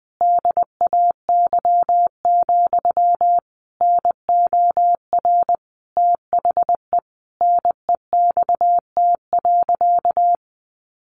Morse code, heard as a keyed tone, about 20 wpm, 700 Hz. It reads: DAY, NOR THE NEXT.
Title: Sentences from Top 1500 Words 20wpm